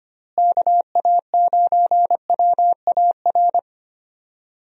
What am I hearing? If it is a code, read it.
Morse code, heard as a keyed tone, about 25 wpm, 700 Hz. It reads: KA9WAR
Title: USA Call Signs - 6 Characters - Repeated 2x - 25wpm